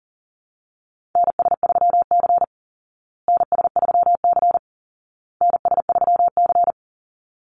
Morse code, heard as a keyed tone, about 40 wpm, 700 Hz. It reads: NS3C NS3C NS3C
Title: Call Signs Repeated 2x - 40wpm